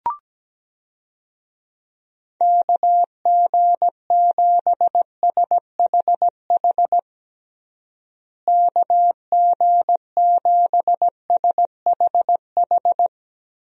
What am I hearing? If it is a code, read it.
Morse code, heard as a keyed tone, about 17 wpm, 700 Hz. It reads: KG7SHH KG7SHH
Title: USA Call Signs - 6 Characters - Repeated 3x - 17wpm